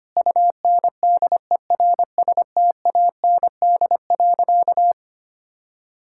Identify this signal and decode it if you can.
Morse code, heard as a keyed tone, about 25 wpm, 700 Hz. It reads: UNDERSTAND.
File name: practice / 25wpm / Sentences with 4-Word Phrases - 25wpm